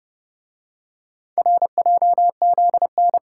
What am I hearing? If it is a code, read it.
Morse code, heard as a keyed tone, about 29 wpm, 700 Hz. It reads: RJZN